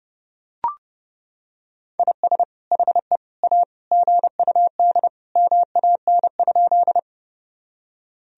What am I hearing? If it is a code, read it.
Morse code, heard as a keyed tone, about 32 wpm, 700 Hz. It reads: IS HE A GUD MAN?
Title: Sentences from Top 200 Words and Abbreviations 30wpm